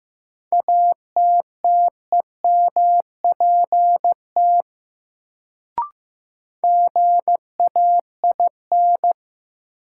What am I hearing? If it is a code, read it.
Morse code, heard as a keyed tone, about 15 wpm, 700 Hz. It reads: ATTEMPT GAIN
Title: Sentences from Top 1000 Words - New Word Drill - 15wpm